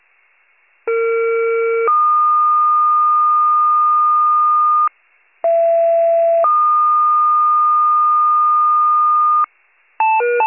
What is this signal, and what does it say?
Two-tone page, heard as a one-second tone, then a three-second tone, about 460 then 1220 Hz, about 670 then 1220 Hz. It is heard from 0.9 to 9.4 s.